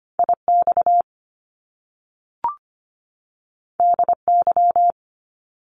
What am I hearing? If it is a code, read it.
Morse code, heard as a keyed tone, about 25 wpm, 700 Hz. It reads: IX DY
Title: Top Two Letter Combinations in Words 25wpm